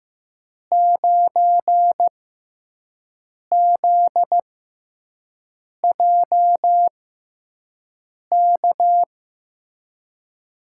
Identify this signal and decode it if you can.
Morse code, heard as a keyed tone, about 15 wpm, spaced as for 5 wpm, 700 Hz. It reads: 9ZJK
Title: USA Call Signs - 5 Characters - 6x Character Spacing - 15wpm